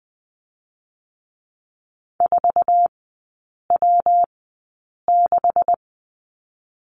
Morse code, heard as a keyed tone, about 20 wpm, spaced as for 8 wpm, 700 Hz. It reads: 4W6
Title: 205 Lesson 33e - Letter j - Three Characters 20wpm